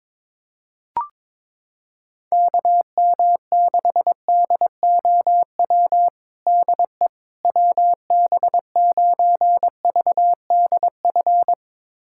Morse code, heard as a keyed tone, about 22 wpm, 700 Hz. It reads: KM6DOW DE WB9VDF
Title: USA Callsign DE Callsign - 6 Characters - 22wpm